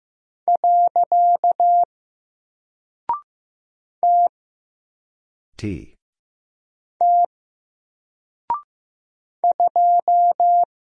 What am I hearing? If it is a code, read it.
Morse code, heard as a keyed tone, about 15 wpm, 700 Hz. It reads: . T T 2
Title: Single Character (Letter:Number:Punctuation:Prosign) Alt 15wpm